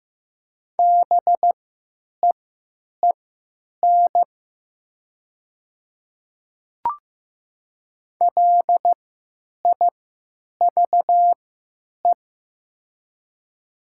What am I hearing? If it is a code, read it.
Morse code, heard as a keyed tone, about 15 wpm, spaced as for 9 wpm, 700 Hz. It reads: BEEN LIVE
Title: 3x Character Spacing - Top 500 Words - 15wpm